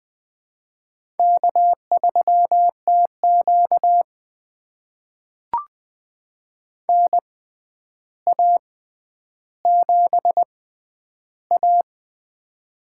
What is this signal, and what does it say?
Morse code, heard as a keyed tone, about 21 wpm, spaced as for 10 wpm, 700 Hz. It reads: K3TQ NA7A
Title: USA Call Signs - 4 Characters - 6x Character Spacing - 20wpm